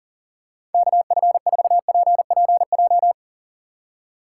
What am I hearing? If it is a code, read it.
Morse code, heard as a keyed tone, about 40 wpm, 700 Hz. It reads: KF4PPJ
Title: USA Call Signs - 6 Characters - 2x Character Spacing - 40wpm